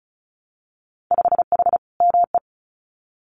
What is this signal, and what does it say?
Morse code, heard as a keyed tone, about 36 wpm, 700 Hz. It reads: 5H ME
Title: Winter Field Day Exchanges Encore 35wpm